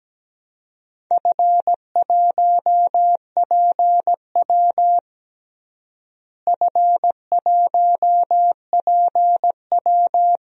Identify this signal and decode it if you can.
Morse code, heard as a keyed tone, about 17 wpm, 700 Hz. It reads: F1PW F1PW
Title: DXCC-100 Call Signs - 4 Characters - Repeated 2x - 17wpm